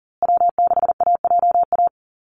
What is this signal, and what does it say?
Morse code, heard as a keyed tone, about 40 wpm, 700 Hz. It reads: W6AJA
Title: USA Call Signs Repeated 2x 40wpm